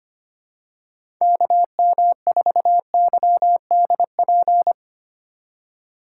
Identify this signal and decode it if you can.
Morse code, heard as a keyed tone, about 25 wpm, 700 Hz. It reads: KM4YDP
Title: USA Call Signs - 2x Character Spacing - 25wpm